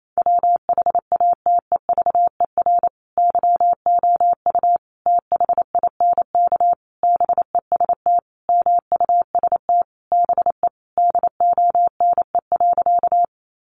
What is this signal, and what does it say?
Morse code, heard as a keyed tone, about 28 wpm, 700 Hz. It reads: WHATEVER YOU THINK BEST MUST BE DONE.
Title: Sentences from Top 1100 Words 28wpm